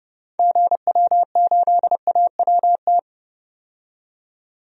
Morse code, heard as a keyed tone, about 30 wpm, 700 Hz. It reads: GW8AWT
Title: DXCC-100 Call Signs - 6 Characters - 30wpm